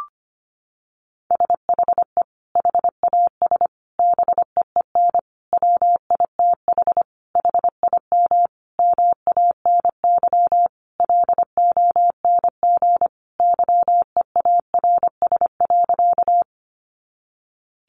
Morse code, heard as a keyed tone, about 25 wpm, 700 Hz. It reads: SHE HAS BEEN WITH HIM MANY LONG YEARS.
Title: Sentences from Top 200 Words 25wpm